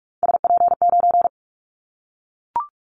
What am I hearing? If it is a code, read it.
Morse code, heard as a keyed tone, about 44 wpm, 700 Hz. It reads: SP9
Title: Callsign Prefixes for Top 25 DXCC - Repeated 3x - 45wpm